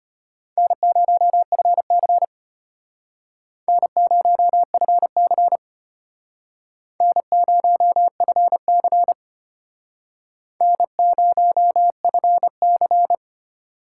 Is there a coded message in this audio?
Yes: Morse code from 0.6 to 13.2 s, heard as a keyed tone, about 34 wpm, 700 Hz. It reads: N0FC N0FC N0FC N0FC